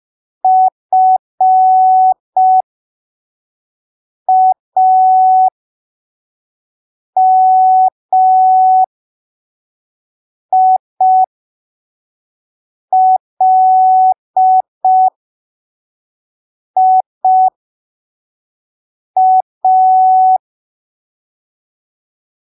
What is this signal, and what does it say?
Morse code, heard as a keyed tone, about 5 wpm, spaced as for 3 wpm, 750 Hz. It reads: FAMILIA